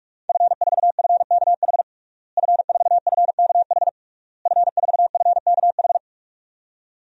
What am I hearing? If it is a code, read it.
Morse code, heard as a keyed tone, about 42 wpm, 700 Hz. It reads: F4FKH F4FKH F4FKH
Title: DXCC-100 Call Signs - 5 Characters - Repeated 3x - 45wpm